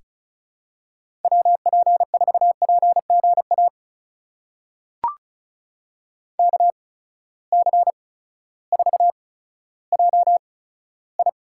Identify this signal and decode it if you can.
Morse code, heard as a keyed tone, about 34 wpm, spaced as for 16 wpm, 700 Hz. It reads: WP4PGA KC4JI
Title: USA Call Signs - 6 Characters - 8x Character Spacing - 35wpm